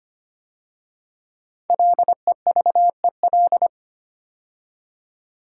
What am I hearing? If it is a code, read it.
Morse code, heard as a keyed tone, about 25 wpm, 700 Hz. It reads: LEVEL
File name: 5 Letter Words - First 3 Letters Given - 25wpm